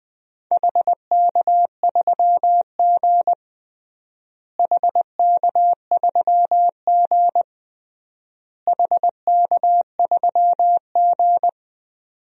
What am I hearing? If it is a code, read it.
Morse code, heard as a keyed tone, about 20 wpm, 700 Hz. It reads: HK3G HK3G HK3G
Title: DXCC-100 Call Signs - 4 Characters - Repeated 3x - 20wpm